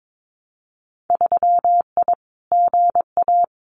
Morse code, heard as a keyed tone, about 22 wpm, 700 Hz. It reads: 3I GA
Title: Winter Field Day Exchanges 22wpm